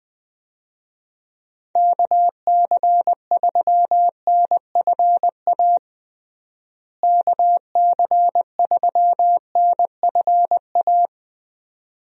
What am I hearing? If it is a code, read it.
Morse code, heard as a keyed tone, about 20 wpm, 700 Hz. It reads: KC3NFA KC3NFA
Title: USA Call Signs - 6 Characters - Repeated 2x - 20wpm